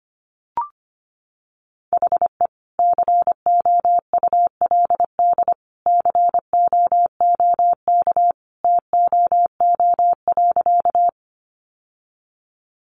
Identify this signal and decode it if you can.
Morse code, heard as a keyed tone, about 25 wpm, 700 Hz. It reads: HE COULD COOK TOO.
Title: Sentences from Top 700 Words 25wpm